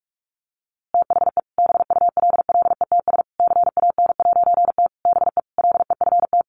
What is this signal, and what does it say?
Morse code, heard as a keyed tone, about 46 wpm, 700 Hz. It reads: THE BULLETS CAN'T BE LEFT
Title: A Sound Of Thunder 45wpm